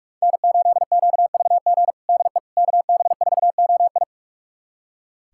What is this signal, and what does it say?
Morse code, heard as a keyed tone, about 46 wpm, 700 Hz. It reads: N8QVG DE KB4OI
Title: USA Callsign DE Callsign - 5 Characters - 45wpm